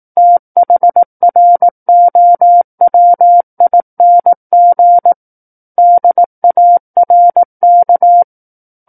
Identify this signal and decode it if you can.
Morse code, heard as a keyed tone, about 18 wpm, 700 Hz. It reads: THROWING DARK
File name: 18 WPM